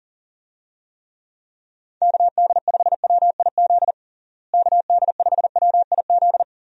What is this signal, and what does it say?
Morse code, heard as a keyed tone, about 40 wpm, 700 Hz. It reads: KD5WIZ KD5WIZ
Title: USA Call Signs - 6 Characters - Repeated 2x - 40wpm